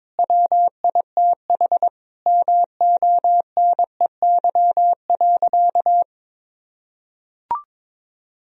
Morse code, heard as a keyed tone, about 22 wpm, 700 Hz. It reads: WITH MONEY.
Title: Sentences from Top 1000 Words - Review Part 2 - 22wpm